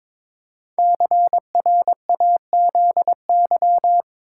Morse code, heard as a keyed tone, about 22 wpm, 700 Hz. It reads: CRAZY
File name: 5 Letter Words - First 3 Letters Given - 22wpm